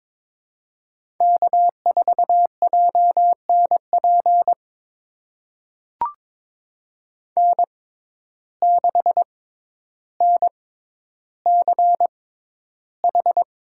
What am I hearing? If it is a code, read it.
Morse code, heard as a keyed tone, about 22 wpm, spaced as for 11 wpm, 700 Hz. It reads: K4JNP N6NCH
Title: USA Call Signs - 6x Character Spacing - 22wpm